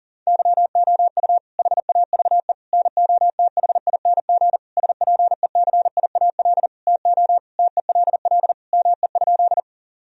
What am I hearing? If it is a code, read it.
Morse code, heard as a keyed tone, about 40 wpm, 700 Hz. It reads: YOU HAVE NOTHING SPECIAL TO TELL ME?
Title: Sentences from Top 1000 Words - Review Part 3 - 40wpm